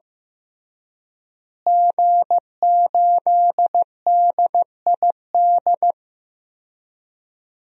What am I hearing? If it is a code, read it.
Morse code, heard as a keyed tone, about 16 wpm, 700 Hz. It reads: G8DID
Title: DXCC-100 Call Signs - 5 Characters - Repeated 3x - 15wpm